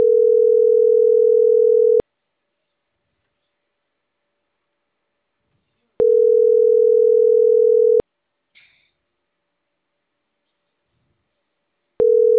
Ringback tone (North America), 440 + 480 Hz.